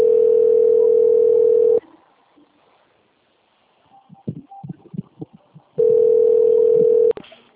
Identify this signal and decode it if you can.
Ringback tone (North America), 440 + 480 Hz.
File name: out- - - - - .21203